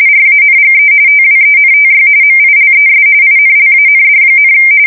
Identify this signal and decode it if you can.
RTTY, about 45.45 baud. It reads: RANGE TWENTY MILES INLAND,
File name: fsk170 / battleshipnj